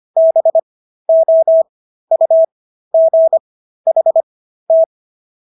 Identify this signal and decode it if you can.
Morse code, heard as a keyed tone, about 25 wpm, spaced as for 13 wpm, 650 Hz. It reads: BOUGHT
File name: SS106 13